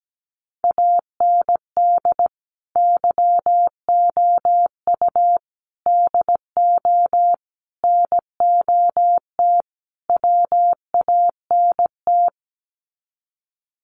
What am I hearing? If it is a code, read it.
Morse code, heard as a keyed tone, about 17 wpm, 700 Hz. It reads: AND YOU DO NOT WANT
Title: Sets of 5 Words Encore2 17wpm